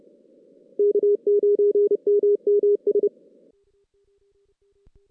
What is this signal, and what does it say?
Morse code, heard as a keyed tone, about 30 wpm, 410 Hz. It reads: K9MMS
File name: 2016 JAN CW NAQP / 2016 Jan naqp k5ot - 0-5